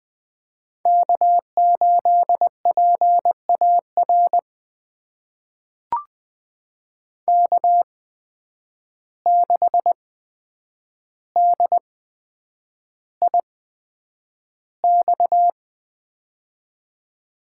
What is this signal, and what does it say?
Morse code, heard as a keyed tone, about 20 wpm, spaced as for 9 wpm, 700 Hz. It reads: K8PAR K6DIX
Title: USA Call Signs - 5 Characters - 8x Character Spacing - 20wpm